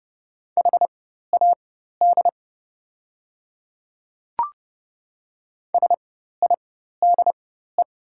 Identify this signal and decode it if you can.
Morse code, heard as a keyed tone, about 30 wpm, spaced as for 14 wpm, 690 Hz. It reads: HAD SIDE
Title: 4x Character Spacing - Top 100 Words - 30wpm